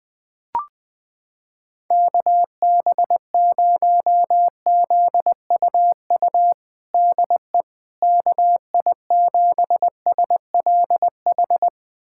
Morse code, heard as a keyed tone, about 21 wpm, 700 Hz. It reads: KB0ZUU DE KI7SLH
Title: USA Callsign DE Callsign - 6 Characters - 20wpm